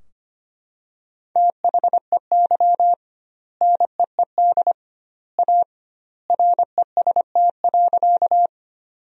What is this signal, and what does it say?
Morse code, heard as a keyed tone, about 25 wpm, 700 Hz. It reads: THEY NEED A REST.
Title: 2x Word-Spacing - Sentences from Top 500 Words - 25wpm